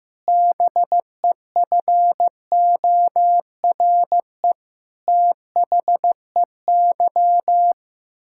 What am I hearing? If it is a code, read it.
Morse code, heard as a keyed tone, about 16 wpm, 700 Hz. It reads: BEFORE THEY
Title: Pangram Challenge v2 - 15wpm